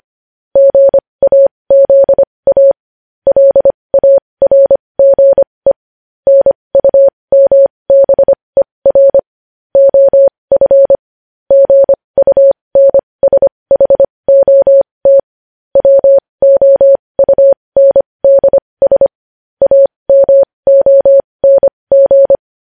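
Morse code, heard as a keyed tone, about 25 wpm, spaced as for 20 wpm, 550 Hz. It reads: GAZA LARGE NUMBER OF GUNSHOT WOUNDS AMONG
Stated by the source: Morse code news headlines sent at 20.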